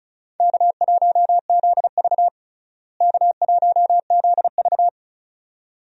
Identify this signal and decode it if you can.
Morse code, heard as a keyed tone, about 36 wpm, 700 Hz. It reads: K1ZV K1ZV
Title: USA Call Signs - 4 Characters - Repeated 2x - 35wpm